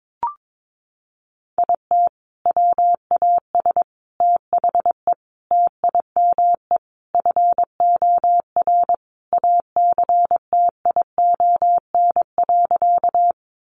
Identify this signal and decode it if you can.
Morse code, heard as a keyed tone, about 22 wpm, 700 Hz. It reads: IT WAS THE TIME FOR ACTION.
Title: Sentences from Top 700 Words 22wpm